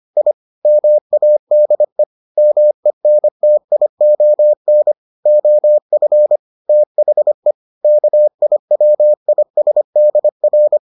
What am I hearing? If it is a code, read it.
Morse code, heard as a keyed tone, about 25 wpm, 600 Hz. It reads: I MADE MENTION OF THE KIWISDR